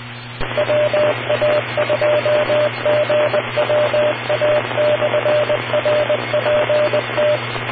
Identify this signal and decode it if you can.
Morse code, heard as a keyed tone, about 20 wpm, 600 Hz, with radio noise. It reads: WA2GWA/RPT